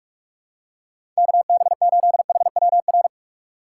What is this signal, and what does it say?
Morse code, heard as a keyed tone, about 46 wpm, 690 Hz. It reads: KB8HWR
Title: USA Call Signs - 8x Character Spacing - 45wpm